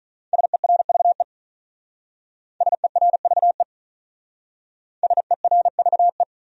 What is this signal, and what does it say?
Morse code, heard as a keyed tone, about 44 wpm, 700 Hz. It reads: SERVE SERVE SERVE